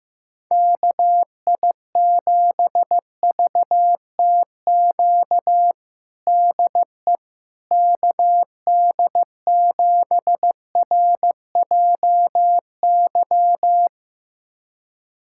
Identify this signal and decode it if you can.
Morse code, heard as a keyed tone, about 15 wpm, 700 Hz. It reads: KI7VTQ DE KD7RJY
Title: USA Callsign DE Callsign - 6 Characters - 15wpm